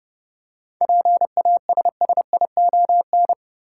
Morse code, heard as a keyed tone, about 31 wpm, 700 Hz. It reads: PASSION
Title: Seven Letter Words - 30wpm